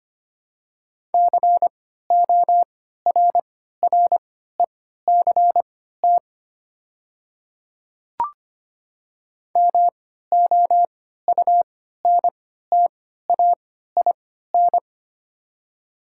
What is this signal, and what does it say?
Morse code, heard as a keyed tone, about 25 wpm, spaced as for 14 wpm, 700 Hz. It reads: CORRECT MOUNTAIN
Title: 3x Character Spacing - Top 500 Words - 25wpm